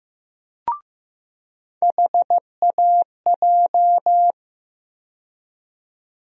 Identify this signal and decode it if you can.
Morse code, heard as a keyed tone, about 15 wpm, 700 Hz. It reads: HAJ
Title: USA Call Sign Postfixes - 3 Letters 15wpm